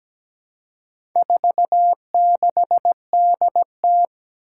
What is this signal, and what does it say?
Morse code, heard as a keyed tone, about 17 wpm, 700 Hz. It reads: 46DT
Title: Four Challenging Letter-Number Combinations 17wpm